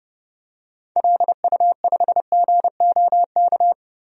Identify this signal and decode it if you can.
Morse code, heard as a keyed tone, about 32 wpm, 700 Hz. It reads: LU5GOK
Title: DXCC-100 Call Signs - 6 Characters - Repeated 3x - 30wpm